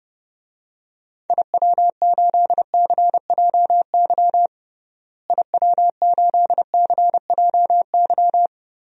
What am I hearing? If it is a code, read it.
Morse code, heard as a keyed tone, about 30 wpm, 700 Hz. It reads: IW8CJY IW8CJY
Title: DXCC-100 Call Signs - 6 Characters - Repeated 2x - 30wpm